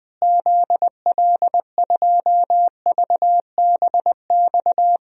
Morse code, heard as a keyed tone, about 20 wpm, 700 Hz. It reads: ZL2VBX
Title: DXCC-100 Call Signs - 6 Characters - Repeated 2x - 20wpm